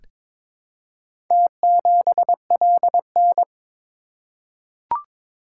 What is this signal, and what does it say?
Morse code, heard as a keyed tone, about 22 wpm, 700 Hz. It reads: T7LN
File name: practice / 22wpm / Four Letter-Number Combinations - 22wpm